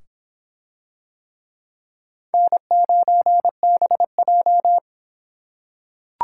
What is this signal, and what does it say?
Morse code, heard as a keyed tone, about 26 wpm, 700 Hz. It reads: N9BJ